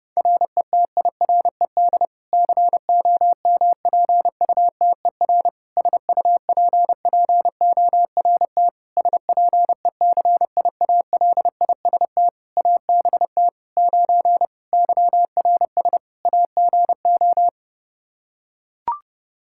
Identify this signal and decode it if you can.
Morse code, heard as a keyed tone, about 30 wpm, 700 Hz. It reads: RETIRED COMPUTER SUPPORT SPECIALIST ABT 9 YRS AGO
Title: Practice QSO - Exchange 3 - More Info - 30wpm